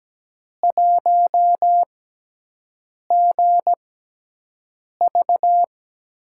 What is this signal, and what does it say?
Morse code, heard as a keyed tone, about 17 wpm, spaced as for 6 wpm, 700 Hz. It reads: 1GV
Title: USA Call Signs - 5 Characters - 6x Character Spacing - 17wpm